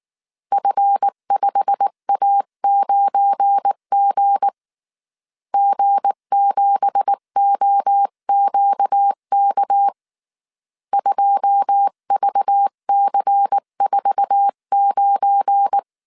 Morse code, heard as a keyed tone, about 19 wpm, 800 Hz. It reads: F5A9G G7OQK 2VC49
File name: CW / DET / 1 9 1500C S C 18wpm 2539